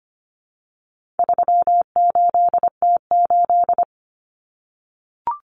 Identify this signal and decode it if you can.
Morse code, heard as a keyed tone, about 25 wpm, 700 Hz. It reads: 38T8
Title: Common Cut Numbers - 4-Digit 25wpm